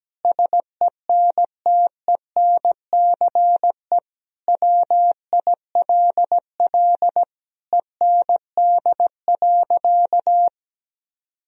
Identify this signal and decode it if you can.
Morse code, heard as a keyed tone, about 17 wpm, 700 Hz. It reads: SENTENCE WILL END.
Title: Sentences from Top 1000 Words - Review Part 1 - 17wpm